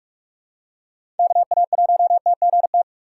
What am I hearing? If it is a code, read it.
Morse code, heard as a keyed tone, about 46 wpm, 700 Hz. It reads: KA1TGT
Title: USA Call Signs - 6 Characters - 2x Character Spacing - 45wpm